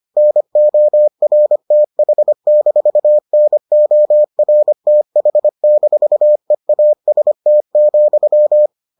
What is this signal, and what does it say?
Morse code, heard as a keyed tone, about 25 wpm, 600 Hz. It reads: NORTH-NORTH-EAST,